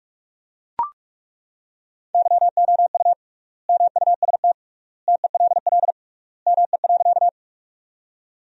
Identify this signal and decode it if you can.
Morse code, heard as a keyed tone, about 42 wpm, 700 Hz. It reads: YOU MUST TELL ME.
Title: 3x Word-Spacing - Sentences from Top 200 Words - 45wpm